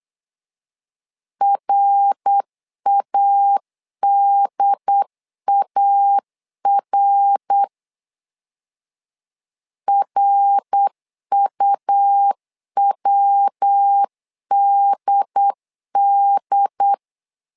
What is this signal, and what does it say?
Morse code, heard as a keyed tone, about 8 wpm, 800 Hz. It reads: RADAR RUWDD
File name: LEZIONI / 3 MWRO / 3 MWR 8wpm 2444